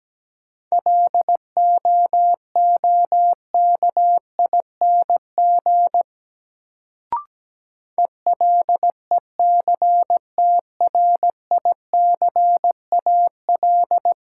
Morse code, heard as a keyed tone, about 17 wpm, 700 Hz. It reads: LOOKING ELECTRICAL
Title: Sentences from Top 1000 Words - Review Part 6 - New Word Drill - 17wpm